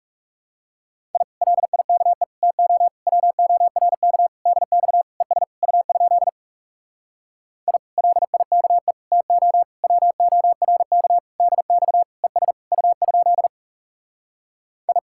Morse code, heard as a keyed tone, about 40 wpm, 700 Hz. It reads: I LIKE TO WORK DX ES U? I LIKE TO WORK DX ES U? I